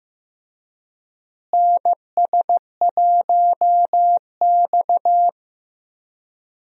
Morse code, heard as a keyed tone, about 15 wpm, 700 Hz. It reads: NS1X
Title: USA Call Signs - 4 Characters - Repeated 2x - 15wpm